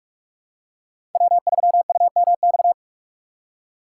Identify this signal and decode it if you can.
Morse code, heard as a keyed tone, about 46 wpm, 700 Hz. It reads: W3UMX
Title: USA Call Signs - 5 Characters - 3x Character Spacing - 45wpm